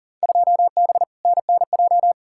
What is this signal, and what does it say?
Morse code, heard as a keyed tone, about 40 wpm, 700 Hz. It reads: 2B NNJ
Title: Field Day Exchanges Encore - 40wpm